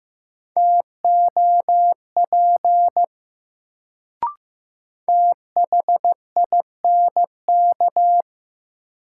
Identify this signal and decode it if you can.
Morse code, heard as a keyed tone, about 15 wpm, 700 Hz. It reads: TOP THINK
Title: Sets of 2 Words - New Word Drill - 15wpm